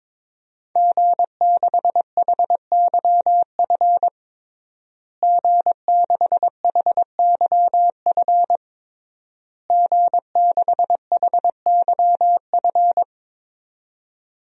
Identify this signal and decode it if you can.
Morse code, heard as a keyed tone, about 22 wpm, 700 Hz. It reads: G6HYF G6HYF G6HYF
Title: DXCC-100 Call Signs - 5 Characters - Repeated 3x - 22wpm